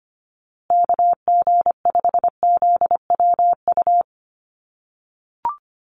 Morse code, heard as a keyed tone, about 25 wpm, 700 Hz. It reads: KG5ZWU
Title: USA Call Signs Repeated 3x 25wpm